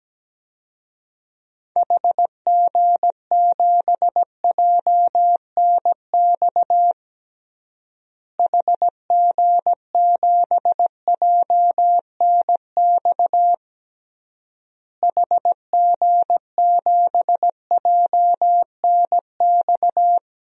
Morse code, heard as a keyed tone, about 17 wpm, 700 Hz. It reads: HG7JNX HG7JNX HG7JNX
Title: DXCC-100 Call Signs - 6 Characters - Repeated 3x - 17wpm